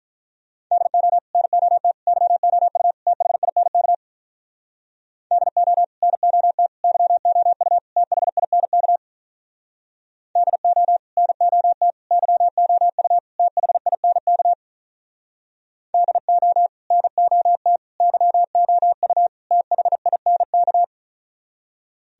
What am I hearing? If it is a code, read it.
Morse code, heard as a keyed tone, about 47 wpm, 700 Hz. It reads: DO NOT YOU THINK DO NOT YOU THINK DO NOT YOU THINK DO NOT YOU THINK